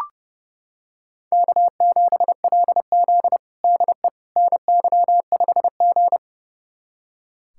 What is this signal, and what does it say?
Morse code, heard as a keyed tone, about 30 wpm, 700 Hz. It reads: K7LZ DE NY5G
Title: USA Callsign DE Callsign - 4 Characters - 30wpm